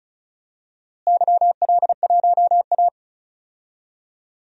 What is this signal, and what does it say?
Morse code, heard as a keyed tone, about 36 wpm, 700 Hz. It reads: YL1A
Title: DXCC-100 Call Signs - 4 Characters - Repeated 3x - 35wpm